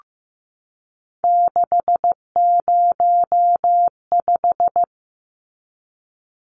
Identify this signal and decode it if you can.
Morse code, heard as a keyed tone, about 15 wpm, 700 Hz. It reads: 605
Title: Numbers - 3-Digit 15wpm